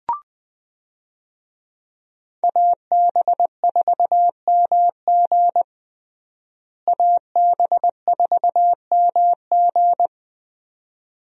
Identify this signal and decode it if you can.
Morse code, heard as a keyed tone, about 21 wpm, 700 Hz. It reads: AB4MG AB4MG
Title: USA Call Signs - 5 Characters - Repeated 2x - 20wpm